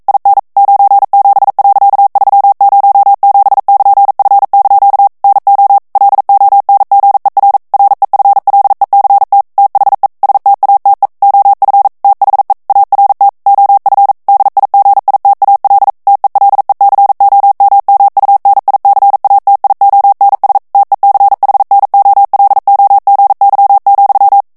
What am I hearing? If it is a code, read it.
Morse code, heard as a keyed tone, about 42 wpm, 800 Hz. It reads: IN 97.307<KN>F) NO LONGER REFLECT THE STATE OF THE ART OF DIGITAL TELECOMMUNICATIONS TECHNOLOGY,